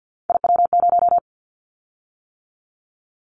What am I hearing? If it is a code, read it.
Morse code, heard as a keyed tone, about 50 wpm, 700 Hz. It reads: IW0
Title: Callsign Prefixes for Top 50 DXCC - 50wpm